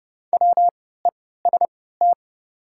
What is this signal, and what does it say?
Morse code, heard as a keyed tone, about 30 wpm, spaced as for 17 wpm, 700 Hz. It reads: WEST
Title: 3x Character Spacing - Top 500 Words - 30wpm